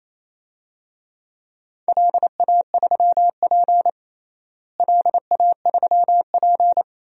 Morse code, heard as a keyed tone, about 28 wpm, 700 Hz. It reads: LA3P LA3P
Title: DXCC-100 Call Signs - 4 Characters - Repeated 3x - 28wpm